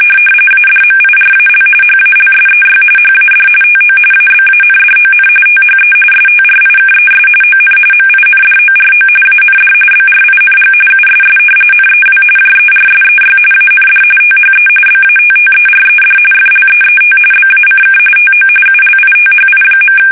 RTTY, about 45.45 baud. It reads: ADMIRAL HALSEY HAD TO ABANDON THE STRIKES ON MINDORO AND BRING USS NEW JERSEY AND THE BATTERED FLEET BACK TO ULITHI.
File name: fsk850 / battleshipnj